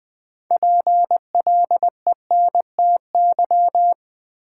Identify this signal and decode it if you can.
Morse code, heard as a keyed tone, about 20 wpm, 700 Hz. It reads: PLENTY
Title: Sentences from Top 1300 Words - New Word Drill Alt - 20wpm